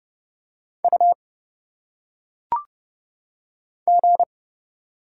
Morse code, heard as a keyed tone, about 29 wpm, 700 Hz. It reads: U G
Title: Single Letters - Phonetic Pronunciation 30wpm